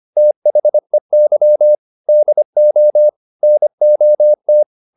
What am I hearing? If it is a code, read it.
Morse code, heard as a keyed tone, about 25 wpm, 600 Hz. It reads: THEY DO NOT